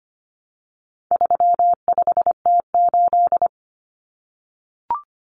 Morse code, heard as a keyed tone, about 25 wpm, 700 Hz. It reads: 35T8
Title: Common Cut Numbers - 4-Digit 25wpm